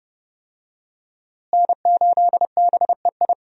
Morse code, heard as a keyed tone, about 30 wpm, 700 Hz. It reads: N8BEI
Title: USA Call Signs - 5 Characters - Repeated 3x - 30wpm